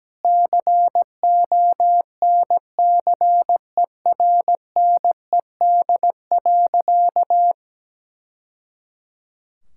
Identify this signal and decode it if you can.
Morse code, heard as a keyed tone, about 17 wpm, 700 Hz. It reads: CONCERNED.
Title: Sentences from Top 1000 Words - Review Part 6 - 17wpm